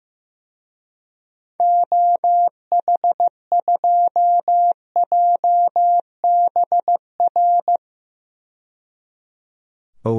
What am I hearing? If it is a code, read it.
Morse code, heard as a keyed tone, about 15 wpm, 700 Hz. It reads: OH2JBR
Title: DXCC-100 Call Signs - 6 Characters - 15wpm